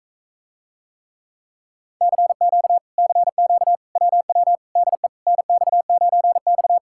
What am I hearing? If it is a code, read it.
Morse code, heard as a keyed tone, about 46 wpm, 700 Hz. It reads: CQ CQ WW DE NX9X